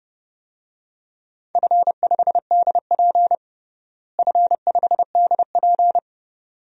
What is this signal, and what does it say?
Morse code, heard as a keyed tone, about 30 wpm, 700 Hz. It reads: F5DP F5DP
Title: DXCC-100 Call Signs - 4 Characters - Repeated 2x - 30wpm